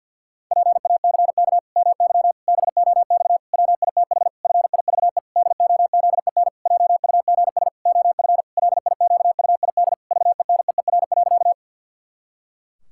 Morse code, heard as a keyed tone, about 50 wpm, 700 Hz. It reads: PACK MY BOX WITH FIVE DOZEN JUGS OF LIQUID VENEER.
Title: Pangram Challenge v2 - 50wpm